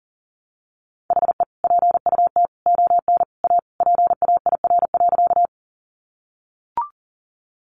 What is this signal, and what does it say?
Morse code, heard as a keyed tone, about 40 wpm, 700 Hz. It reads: HE PUT ON A PAIR.
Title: Sentences from Top 1200 Words 40wpm